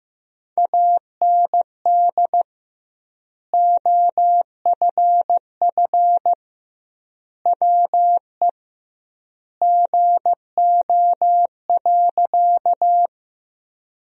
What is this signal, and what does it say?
Morse code, heard as a keyed tone, about 16 wpm, 700 Hz. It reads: AND OFF WE GO.
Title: 2x Word-Spacing - Sentences from Top 200 Words - 15wpm